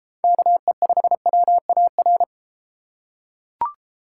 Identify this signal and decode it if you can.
Morse code, heard as a keyed tone, about 32 wpm, 700 Hz. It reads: KE5WAR